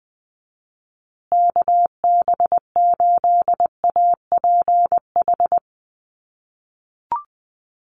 Morse code, heard as a keyed tone, about 20 wpm, 700 Hz. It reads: KB8APH
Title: USA Call Signs Repeated 3x 20wpm